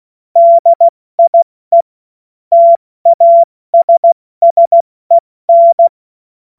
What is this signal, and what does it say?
Morse code, heard as a keyed tone, about 16 wpm, spaced as for 14 wpm, 680 Hz. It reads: DIE TASSEN
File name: Morsekurs / Witze / Witz 07. September 2021, 70 F80